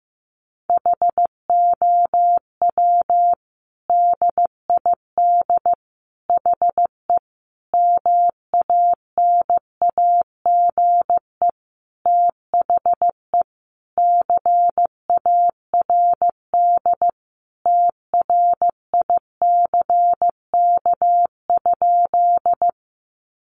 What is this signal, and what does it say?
Morse code, heard as a keyed tone, about 15 wpm, 700 Hz. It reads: HOW DID HE MANAGE THE CARD TRICK?
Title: Sentences from Top 1500 Words 15wpm